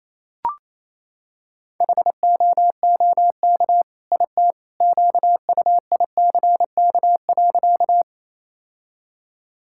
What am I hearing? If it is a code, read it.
Morse code, heard as a keyed tone, about 28 wpm, 700 Hz. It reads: HOOK IT QUICK.
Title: Sentences from Top 1600 Words 28wpm